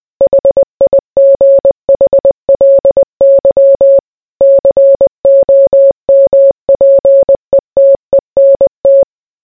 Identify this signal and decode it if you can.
Morse code, heard as a keyed tone, about 20 wpm, 550 Hz. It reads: HIGHLY COMPETENT